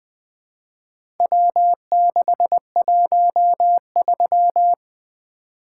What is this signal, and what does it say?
Morse code, heard as a keyed tone, about 20 wpm, 700 Hz. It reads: W613
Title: Four Letter-Number Combinations - 20wpm